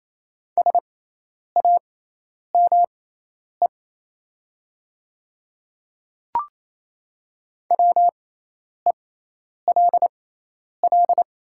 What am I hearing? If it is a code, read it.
Morse code, heard as a keyed tone, about 28 wpm, spaced as for 10 wpm, 700 Hz. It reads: SAME WELL
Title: 6x Character Spacing - Top 300 Words - 28wpm